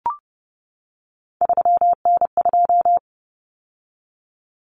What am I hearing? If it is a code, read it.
Morse code, heard as a keyed tone, about 30 wpm, 700 Hz. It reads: 3N2
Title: Common Cut Numbers - 3-Digit 30wpm